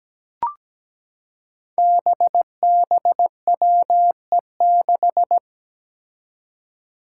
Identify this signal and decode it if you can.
Morse code, heard as a keyed tone, about 17 wpm, 700 Hz. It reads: BBWE6
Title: Five Challenging Letter-Number Combinations 17wpm